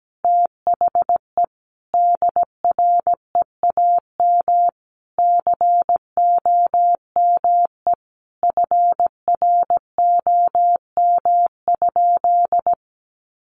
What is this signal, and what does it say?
Morse code, heard as a keyed tone, about 17 wpm, 700 Hz. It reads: THE DREAM COME FROM?
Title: Sentences from Top 1500 Words 17wpm